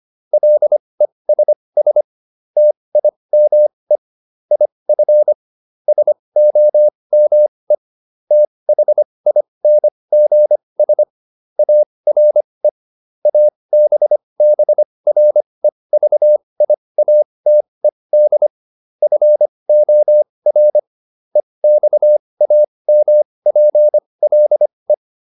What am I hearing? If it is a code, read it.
Morse code, heard as a keyed tone, about 25 wpm, spaced as for 20 wpm, 600 Hz. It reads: LESS TIME IF SOME THINGS ARE ABBREVIATED FOR EXAMPLE